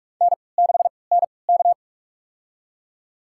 Morse code, heard as a keyed tone, about 45 wpm, spaced as for 25 wpm, 690 Hz. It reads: N6NK
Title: USA Call Signs - 4 Characters - 3x Character Spacing - 45wpm